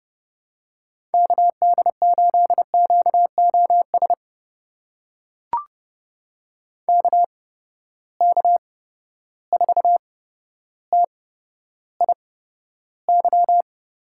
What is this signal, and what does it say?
Morse code, heard as a keyed tone, about 30 wpm, spaced as for 13 wpm, 700 Hz. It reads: KD8QOS KK4TIY
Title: USA Call Signs - 8x Character Spacing - 30wpm